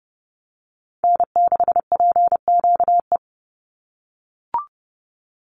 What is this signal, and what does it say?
Morse code, heard as a keyed tone, about 30 wpm, 700 Hz. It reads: N6PQE
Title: USA Call Signs 30wpm